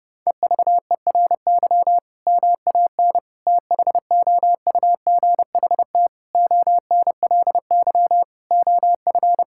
Morse code, heard as a keyed tone, about 32 wpm, 700 Hz. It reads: EVERY MAN THOUGHT ONLY OF
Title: Sentences from Top 1600 Words 30wpm